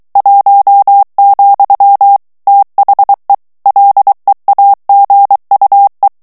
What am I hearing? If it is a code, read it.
Morse code, heard as a keyed tone, about 23 wpm, 800 Hz. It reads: 1, THE LEAGUE
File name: ANGLAIS texte clair 3 vitesse 22 mots minute